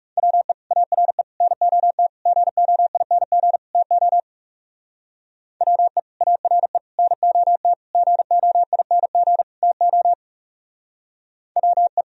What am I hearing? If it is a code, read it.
Morse code, heard as a keyed tone, about 38 wpm, 700 Hz. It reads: WE ARE NOT GOING TO WE ARE NOT GOING TO WE